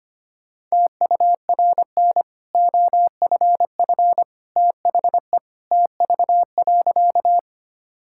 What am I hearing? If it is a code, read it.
Morse code, heard as a keyed tone, about 25 wpm, 700 Hz. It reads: TURN OFF THE TV.
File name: Easy Sentences with Hints - 25wpm